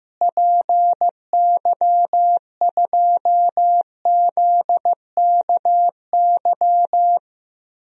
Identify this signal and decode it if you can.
Morse code, heard as a keyed tone, about 16 wpm, 700 Hz. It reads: PY2ZKY
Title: DXCC-100 Call Signs - 6 Characters - Repeated 3x - 15wpm